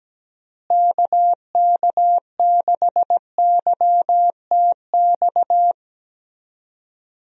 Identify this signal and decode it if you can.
Morse code, heard as a keyed tone, about 17 wpm, 700 Hz. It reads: KK6YTX
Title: USA Call Signs - 6 Characters - Repeated 2x - 17wpm